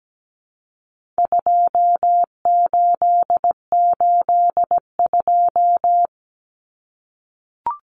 Morse code, heard as a keyed tone, about 17 wpm, 700 Hz. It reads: 2882
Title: Common Cut Numbers - 4-Digit 17wpm